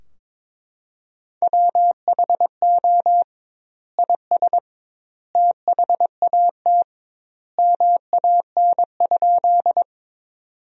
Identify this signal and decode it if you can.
Morse code, heard as a keyed tone, about 22 wpm, 700 Hz. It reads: WHO IS THAT MAN?
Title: 2x Word-Spacing - Sentences from Top 200 Words - 22wpm